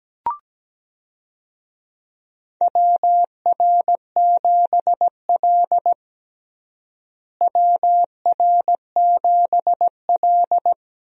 Morse code, heard as a keyed tone, about 17 wpm, 700 Hz. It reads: WR7L WR7L
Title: USA Call Signs - 4 Characters - Repeated 3x - 17wpm